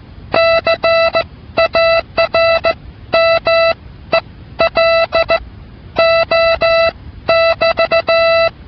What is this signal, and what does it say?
Morse code, heard as a keyed tone, about 16 wpm, spaced as for 12 wpm, 660 Hz. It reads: CARMELO <BT>